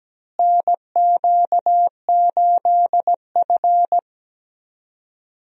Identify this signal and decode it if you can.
Morse code, heard as a keyed tone, about 17 wpm, 700 Hz. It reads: NQ8F
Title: USA Call Signs - 4 Characters - Repeated 3x - 17wpm